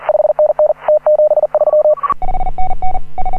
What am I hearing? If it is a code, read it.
Morse code, heard as a keyed tone, about 41 wpm, 610 Hz. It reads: 5NN T73
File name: AUDIO2018 / HA9A CW